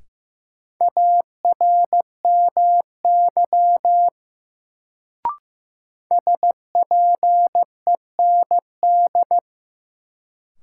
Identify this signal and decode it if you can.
Morse code, heard as a keyed tone, about 15 wpm, 700 Hz. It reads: ARMY SPEND
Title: Sentences from Top 800 Words - New Word Drill - 15wpm